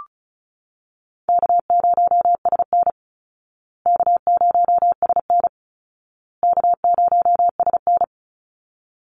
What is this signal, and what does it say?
Morse code, heard as a keyed tone, about 34 wpm, 700 Hz. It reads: K0SN K0SN K0SN
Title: Easy Call Signs Repeated 3x - 35wpm